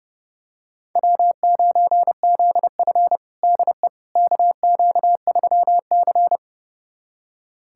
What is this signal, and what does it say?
Morse code, heard as a keyed tone, about 32 wpm, 700 Hz. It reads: W9ZF DE KQ3C
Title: USA Callsign DE Callsign - 4 Characters - 30wpm